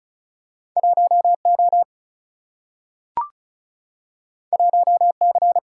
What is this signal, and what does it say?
Morse code, heard as a keyed tone, about 36 wpm, 700 Hz. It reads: 1O 1C
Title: Single-Letter Single-Number 35wpm